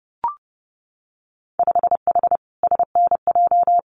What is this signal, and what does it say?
Morse code, heard as a keyed tone, about 30 wpm, 700 Hz. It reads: 5H SNJ
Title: Winter Field Day Exchanges Encore 30wpm